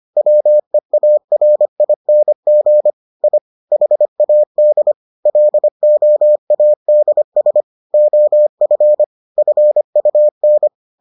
Morse code, heard as a keyed tone, about 25 wpm, 600 Hz. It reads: WEARING I HAD LOADS OF FUN